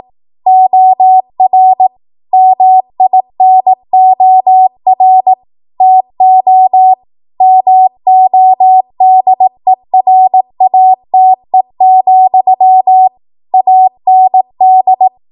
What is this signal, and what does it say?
Morse code, heard as a keyed tone, about 18 wpm, 750 Hz. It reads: OR MINOR TO MODERATE, AND
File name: Morse / 18WPM